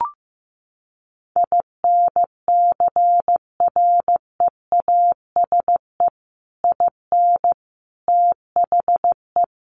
Morse code, heard as a keyed tone, about 15 wpm, 700 Hz. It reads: INCREASE IN THE
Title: Sets of 5 Words 15wpm